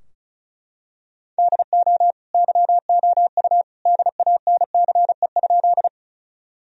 Morse code, heard as a keyed tone, about 36 wpm, 700 Hz. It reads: DO YOU DANCE?
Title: Sentences from Top 1000 Words - Review Part 4 - 35wpm